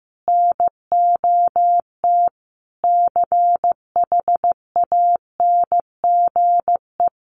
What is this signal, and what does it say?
Morse code, heard as a keyed tone, about 15 wpm, 700 Hz. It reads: NOT CHANGE
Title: Sentences from Top 1400 Words 15wpm